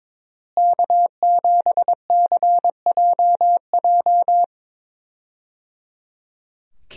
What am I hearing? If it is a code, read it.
Morse code, heard as a keyed tone, about 22 wpm, 700 Hz. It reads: K7CJJ
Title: USA Call Signs - 5 Characters - Repeated 3x - 22wpm